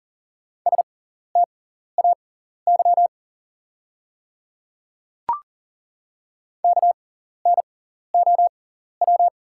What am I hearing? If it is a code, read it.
Morse code, heard as a keyed tone, about 40 wpm, spaced as for 14 wpm, 690 Hz. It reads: STAY KNOW
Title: 6x Character Spacing - Top 500 Words - 40wpm